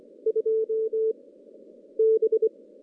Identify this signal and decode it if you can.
Morse code, heard as a keyed tone, about 23 wpm, 440 Hz. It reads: 2 B
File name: contests / CW-SS-k5tr- -2300